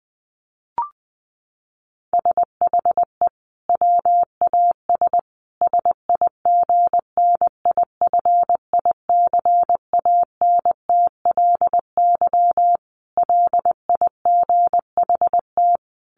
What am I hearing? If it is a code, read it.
Morse code, heard as a keyed tone, about 20 wpm, 700 Hz. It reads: SHE WAS SIGNIFICANTLY LIGHT
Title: Sentences from Top 1500 Words 20wpm